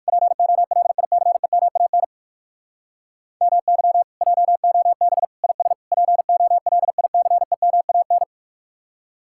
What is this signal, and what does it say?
Morse code, heard as a keyed tone, about 57 wpm, 700 Hz. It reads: POLICEMAN MY JOB IS POLICEMAN